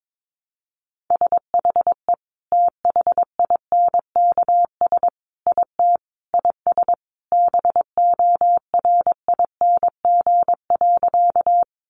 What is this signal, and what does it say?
Morse code, heard as a keyed tone, about 22 wpm, 700 Hz. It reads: SHE THINKS IT IS BORING.
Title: Easy Phrases 22wpm